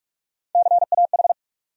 Morse code, heard as a keyed tone, about 42 wpm, 700 Hz. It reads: CAH
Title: USA Call Sign Postfixes - 3 Letters 45wpm